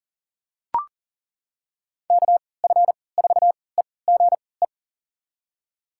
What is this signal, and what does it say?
Morse code, heard as a keyed tone, about 40 wpm, spaced as for 23 wpm, 700 Hz. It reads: KF4EGE
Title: USA Call Signs - 6 Characters - 3x Character Spacing - 40wpm